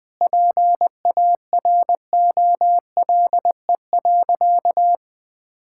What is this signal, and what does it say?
Morse code, heard as a keyed tone, about 20 wpm, 700 Hz. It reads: PAROLE.
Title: Sentences with 8-Word Phrases - 20wpm